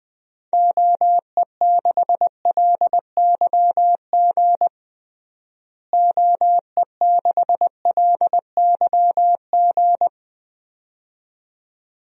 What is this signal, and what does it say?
Morse code, heard as a keyed tone, about 20 wpm, 700 Hz. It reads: OE6LYG OE6LYG
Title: DXCC-100 Call Signs - 6 Characters - Repeated 2x - 20wpm